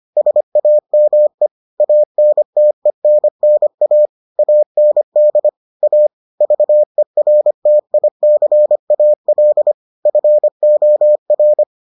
Morse code, heard as a keyed tone, about 25 wpm, 600 Hz. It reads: SAME ANTENNA AND A VERTICAL FOR